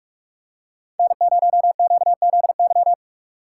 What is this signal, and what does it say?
Morse code, heard as a keyed tone, about 46 wpm, 700 Hz. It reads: N0QZY
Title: USA Call Signs - 5 Characters - 2x Character Spacing - 45wpm